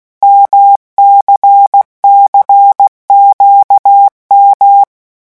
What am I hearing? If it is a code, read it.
Morse code, heard as a keyed tone, about 16 wpm, 800 Hz. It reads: MCCQM
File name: morse mp3 / revision1-vitesse 15 mots